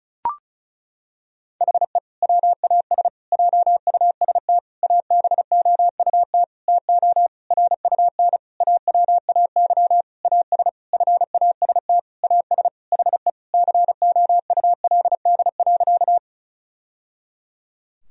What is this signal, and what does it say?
Morse code, heard as a keyed tone, about 34 wpm, 700 Hz. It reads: HE WAS JUST ABOUT TO RUN AWAY AS FAST AS HE COULD.
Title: Sentences from Top 1000 Words - Review Part 2 - 35wpm